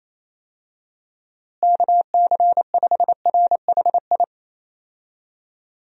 Morse code, heard as a keyed tone, about 28 wpm, 700 Hz. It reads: KC5RHI
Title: USA Call Signs - 6 Characters - Repeated 3x - 28wpm